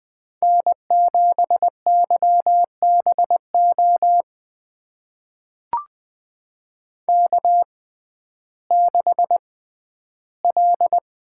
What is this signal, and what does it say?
Morse code, heard as a keyed tone, about 20 wpm, spaced as for 12 wpm, 700 Hz. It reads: N7YBO K6L
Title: USA Call Signs - 5 Characters - 6x Character Spacing - 20wpm